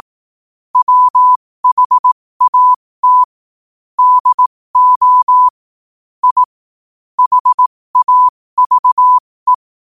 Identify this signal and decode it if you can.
Morse code, heard as a keyed tone, about 18 wpm, spaced as for 15 wpm, 1000 Hz. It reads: WHAT DO I HAVE
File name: qotdfiles / 15wpm